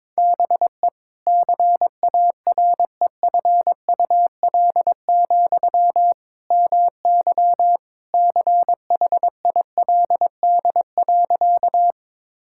Morse code, heard as a keyed tone, about 21 wpm, 700 Hz. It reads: BE CAREFUL, MY CHILD.
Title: Sentences from Top 1600 Words 22wpm